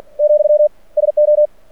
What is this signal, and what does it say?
Morse code, heard as a keyed tone, about 49 wpm, 600 Hz. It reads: , SO
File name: archived / CRSnet